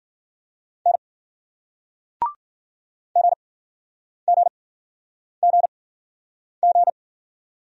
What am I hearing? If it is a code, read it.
Morse code, heard as a keyed tone, about 54 wpm, spaced as for 9 wpm, 700 Hz. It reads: N GGGG